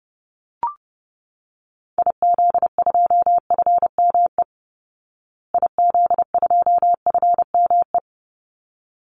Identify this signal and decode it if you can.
Morse code, heard as a keyed tone, about 30 wpm, 700 Hz. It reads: IZ2FME IZ2FME
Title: Call Signs Repeated 3x - 30wpm